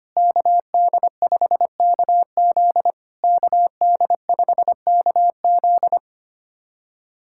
Morse code, heard as a keyed tone, about 25 wpm, 690 Hz. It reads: KD5KZ KD5KZ
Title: SOTA QSO Practice 25wpm